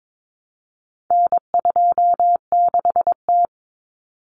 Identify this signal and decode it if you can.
Morse code, heard as a keyed tone, about 22 wpm, 700 Hz. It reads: N26T
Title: Common Cut Numbers - 4-Digit 22wpm